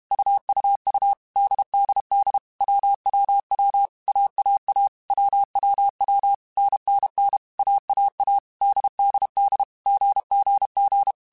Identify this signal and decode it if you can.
Morse code, heard as a keyed tone, about 32 wpm, 800 Hz. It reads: UUU DDD WWW AAA WWW NNN AAA DDD GGG
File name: Basic Sets / Basic Letter Groups Collective7 / AWUNDG@30